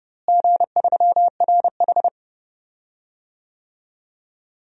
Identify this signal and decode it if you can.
Morse code, heard as a keyed tone, about 30 wpm, 700 Hz. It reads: G3RH
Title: DXCC-100 Call Signs - 4 Characters - Repeated 3x - 30wpm